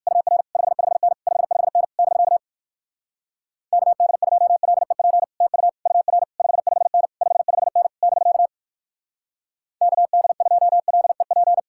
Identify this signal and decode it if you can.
Morse code, heard as a keyed tone, about 56 wpm, 700 Hz. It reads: UR 55N 55N <BK> KD2LEP TU UR 55N 55N <BK> KD2LEP
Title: Speed-Racing Common QSO Phrases - 45wpm